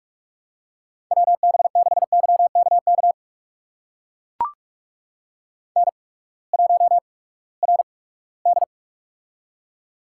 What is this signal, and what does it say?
Morse code, heard as a keyed tone, about 44 wpm, spaced as for 22 wpm, 700 Hz. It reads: WB6YKK N1RD
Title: USA Call Signs - 8x Character Spacing - 45wpm